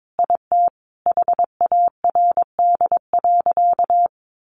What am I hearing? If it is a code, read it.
Morse code, heard as a keyed tone, about 22 wpm, 700 Hz. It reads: IT HARD.
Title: Sentences from Top 1200 Words 22wpm